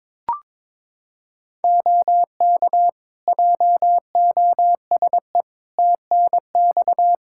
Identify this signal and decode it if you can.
Morse code, heard as a keyed tone, about 22 wpm, 700 Hz. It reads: OK JOSE TNX
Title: K1USN SST QSO Practice 22wpm